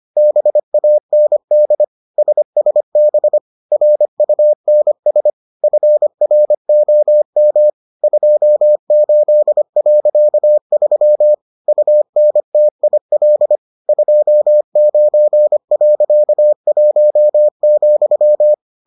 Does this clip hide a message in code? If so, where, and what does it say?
Yes: Morse code from 0.2 to 18.5 s, heard as a keyed tone, about 25 wpm, 600 Hz. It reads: BAND SSB RUNS FROM 28.3 UNTIL 29.1,